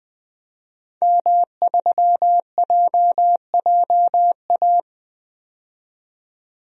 Morse code, heard as a keyed tone, about 20 wpm, 700 Hz. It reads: M3JJA
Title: DXCC-100 Call Signs - 5 Characters - 20wpm